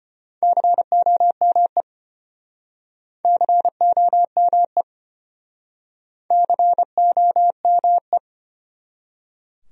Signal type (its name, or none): Morse code